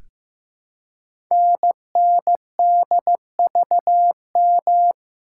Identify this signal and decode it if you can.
Morse code, heard as a keyed tone, about 15 wpm, 700 Hz. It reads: NNDVM
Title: Code Groups - 5 Letters - 15wpm